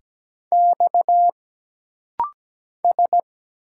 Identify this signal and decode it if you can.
Morse code, heard as a keyed tone, about 17 wpm, 700 Hz. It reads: X S
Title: Single Letter-Number - Rapid-Fire 17wpm